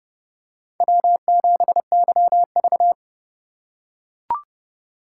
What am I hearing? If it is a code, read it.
Morse code, heard as a keyed tone, about 30 wpm, 700 Hz. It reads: W7YV
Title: USA Call Signs - 4 Characters - 3x Character Spacing - 30wpm